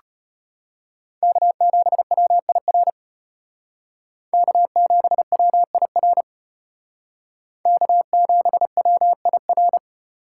Morse code, heard as a keyed tone, about 33 wpm, 700 Hz. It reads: K7WIR K7WIR K7WIR